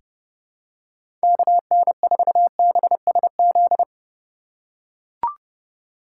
Morse code, heard as a keyed tone, about 32 wpm, 700 Hz. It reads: KN4BSZ
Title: USA Call Signs - 6 Characters - 4x Character Spacing - 30wpm